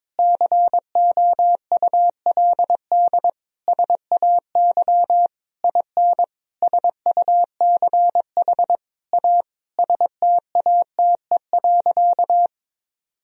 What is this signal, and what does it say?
Morse code, heard as a keyed tone, about 22 wpm, 700 Hz. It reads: COULD SAY IN SUCH A STATE.
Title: Sentences from Top 1000 Words - Review Part 2 - 22wpm